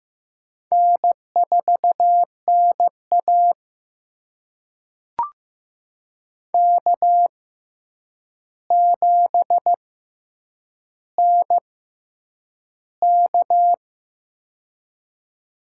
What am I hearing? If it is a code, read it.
Morse code, heard as a keyed tone, about 15 wpm, spaced as for 8 wpm, 700 Hz. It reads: N4NA K7NK
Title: USA Call Signs - 4 Characters - 6x Character Spacing - 15wpm